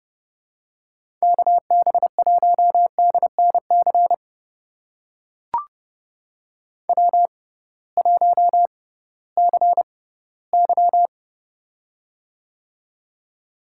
Morse code, heard as a keyed tone, about 32 wpm, spaced as for 18 wpm, 700 Hz. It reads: KB1DNC W1CY
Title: USA Call Signs - 6x Character Spacing - 30wpm